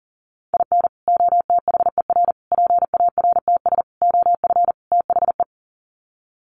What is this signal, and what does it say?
Morse code, heard as a keyed tone, about 40 wpm, 700 Hz. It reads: IN OTHER PARTS OF THE
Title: Sets of 5 Words 40wpm